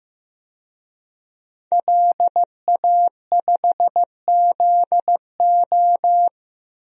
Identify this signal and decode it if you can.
Morse code, heard as a keyed tone, about 15 wpm, 700 Hz. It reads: LA5ZO
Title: DXCC-100 Call Signs - 5 Characters - 15wpm